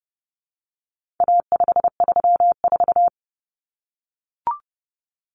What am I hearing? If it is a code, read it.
Morse code, heard as a keyed tone, about 30 wpm, 700 Hz. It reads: A534
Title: Common Cut Numbers - 4-Digit 30wpm